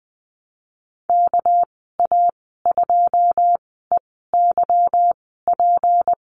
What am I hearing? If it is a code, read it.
Morse code, heard as a keyed tone, about 20 wpm, spaced as for 15 wpm, 700 Hz. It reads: KA2EYP
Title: 200 Lesson 32g - Letter k - Call Signs 20wpm